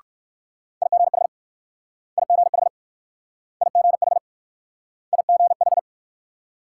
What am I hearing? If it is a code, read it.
Morse code, heard as a keyed tone, about 60 wpm, 700 Hz. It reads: IGH IGH IGH IGH